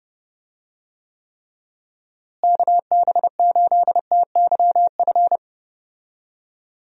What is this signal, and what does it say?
Morse code, heard as a keyed tone, about 32 wpm, 700 Hz. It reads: KB8TYF
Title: USA Call Signs - 6 Characters - Repeated 2x - 30wpm